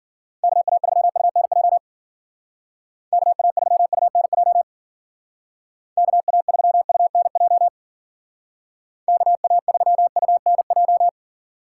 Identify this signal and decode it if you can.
Morse code, heard as a keyed tone, about 48 wpm, 700 Hz. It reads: KA3UNJ KA3UNJ KA3UNJ KA3UNJ
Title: Speed Racing - USA Call Signs 40wpm